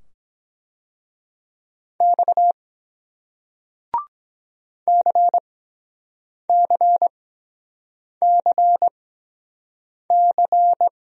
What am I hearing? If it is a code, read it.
Morse code, heard as a keyed tone, about 22 wpm, spaced as for 7 wpm, 700 Hz. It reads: X CCCC